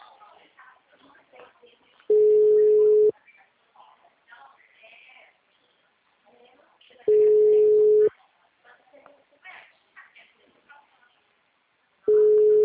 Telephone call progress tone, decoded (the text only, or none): ringback tone